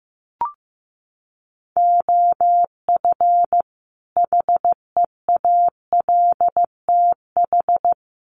Morse code, heard as a keyed tone, about 15 wpm, 700 Hz. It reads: OF HEALTH
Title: Sets of 5 Words 15wpm